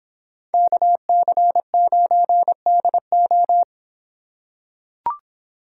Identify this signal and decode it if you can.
Morse code, heard as a keyed tone, about 26 wpm, 700 Hz. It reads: KC9DO